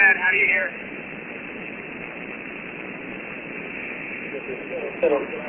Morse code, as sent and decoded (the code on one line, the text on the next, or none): none